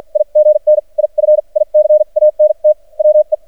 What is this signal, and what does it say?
Morse code, heard as a keyed tone, about 53 wpm, spaced as for 39 wpm, 610 Hz. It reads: IGNIFICANT WE
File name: archived / CRSnet